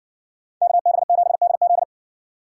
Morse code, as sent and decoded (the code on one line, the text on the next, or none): -.- -... --... -.. --..
KB7DZ